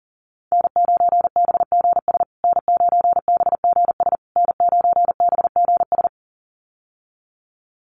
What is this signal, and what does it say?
Morse code, heard as a keyed tone, about 40 wpm, 700 Hz. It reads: N9BGS N9BGS N9BGS
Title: USA Call Signs Repeated 3x 40wpm